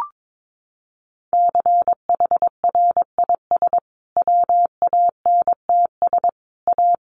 Morse code, heard as a keyed tone, about 22 wpm, 700 Hz. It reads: CHRIS WANTS A
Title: Additional Easy Phrases 22wpm